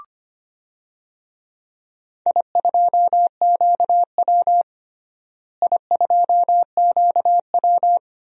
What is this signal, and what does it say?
Morse code, heard as a keyed tone, about 25 wpm, 700 Hz. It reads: I2QW I2QW
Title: DXCC-100 Call Signs - 4 Characters - Repeated 2x - 25wpm